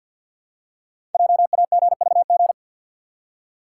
Morse code, heard as a keyed tone, about 51 wpm, 700 Hz. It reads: JAGVG
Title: Code Groups - 5 Letters - 50wpm